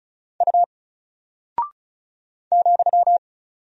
Morse code, heard as a keyed tone, about 36 wpm, 700 Hz. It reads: U ,
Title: Single Character (Letter:Number:Punctuation:Prosign) 35wpm